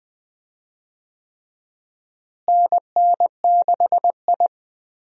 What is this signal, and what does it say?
Morse code, heard as a keyed tone, about 19 wpm, 700 Hz. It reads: NN6I
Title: USA Call Signs - 4 Characters - Repeated 3x - 20wpm